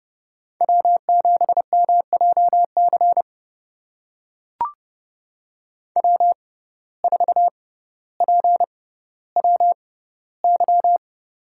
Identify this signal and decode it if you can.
Morse code, heard as a keyed tone, about 30 wpm, spaced as for 15 wpm, 700 Hz. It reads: W7MJC W4PWY
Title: USA Call Signs - 5 Characters - 6x Character Spacing - 30wpm